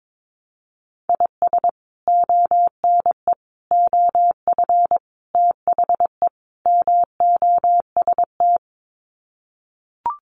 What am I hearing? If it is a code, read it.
Morse code, heard as a keyed tone, about 22 wpm, 700 Hz. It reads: IS ONE OF THE MOST
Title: Sets of 5 Words 22wpm